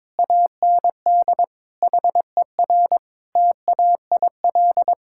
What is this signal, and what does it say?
Morse code, heard as a keyed tone, about 21 wpm, 700 Hz. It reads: AND HER TAIL
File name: practice / 22wpm / Sentences with 2-Word Phrases - 22wpm